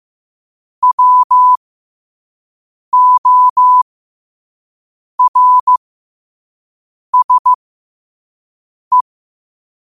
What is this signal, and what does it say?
Morse code, heard as a keyed tone, about 15 wpm, spaced as for 5 wpm, 1000 Hz. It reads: WORSE
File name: qotdfiles / 5wpm